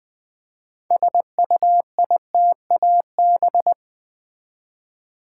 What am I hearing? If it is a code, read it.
Morse code, heard as a keyed tone, about 20 wpm, 700 Hz. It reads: SUITAB
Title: Build A Word - 3 Letters At A Time - 8-Letter Words - 20wpm